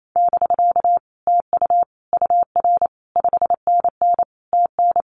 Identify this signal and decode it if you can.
Morse code, heard as a keyed tone, about 28 wpm, 700 Hz. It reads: <BK> TU UR 5NN TN
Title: Common QSO Phrases Encore 28wpm